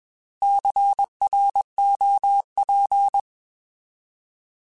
Morse code, heard as a keyed tone, about 21 wpm, 800 Hz. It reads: CROP